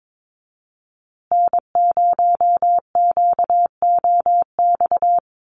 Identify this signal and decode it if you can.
Morse code, heard as a keyed tone, about 22 wpm, 700 Hz. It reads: N0QOX
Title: USA Call Signs Repeated 3x 22wpm